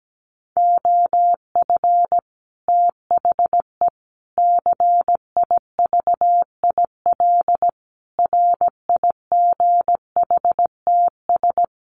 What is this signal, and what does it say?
Morse code, heard as a keyed tone, about 17 wpm, 700 Hz. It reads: OF THE CIVIL RIGHTS
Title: Sets of 5 Words Encore2 17wpm